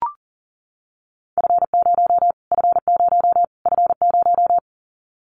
Morse code, heard as a keyed tone, about 40 wpm, 700 Hz. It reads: F0 F0 F0
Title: Callsign Prefixes for Top 25 DXCC - Repeated 3x - 40wpm